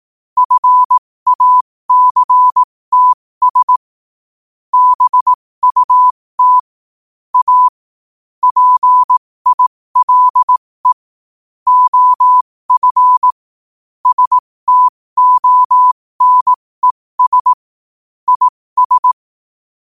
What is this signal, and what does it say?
Morse code, heard as a keyed tone, about 18 wpm, spaced as for 15 wpm, 1000 Hz. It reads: FACTS BUT A PILE OF STONES IS
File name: qotdfiles / 15wpm